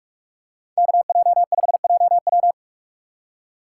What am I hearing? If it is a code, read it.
Morse code, heard as a keyed tone, about 44 wpm, 700 Hz. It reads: KJ5JW